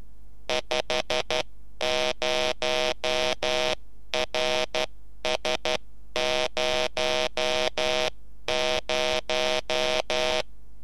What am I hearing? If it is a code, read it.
Morse code, heard as a keyed tone, about 12 wpm, 630 Hz, with noise. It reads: 50RS00
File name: beacon